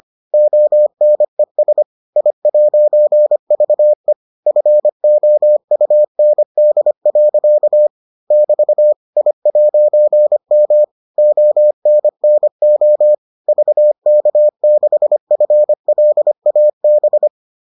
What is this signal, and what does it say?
Morse code, heard as a keyed tone, about 25 wpm, 600 Hz. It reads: ONES I'VE FOUND. <BT> I'M ONNO VK6FLAB